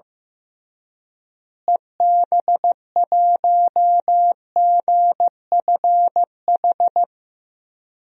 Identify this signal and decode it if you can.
Morse code, heard as a keyed tone, about 15 wpm, 700 Hz. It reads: EB1GFH
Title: DXCC-100 Call Signs - 6 Characters - Repeated 3x - 15wpm